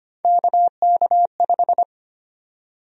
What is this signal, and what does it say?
Morse code, heard as a keyed tone, about 25 wpm, 700 Hz. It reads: KK5
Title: USA Call Sign Prefixes - Repeated 2x - 25wpm